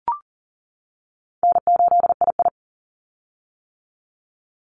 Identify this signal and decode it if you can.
Morse code, heard as a keyed tone, about 40 wpm, 700 Hz. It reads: N8II
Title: Easy Call Signs - 40wpm